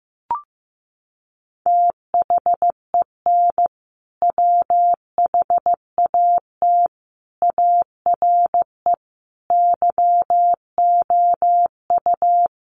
Morse code, heard as a keyed tone, about 15 wpm, 700 Hz. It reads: THEN WHAT ARE YOU
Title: Sentences from Top 100 Words 15wpm